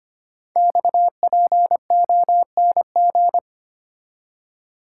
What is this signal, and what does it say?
Morse code, heard as a keyed tone, about 25 wpm, 700 Hz. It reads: XPONG
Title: Code Groups - 5 Letters - 25wpm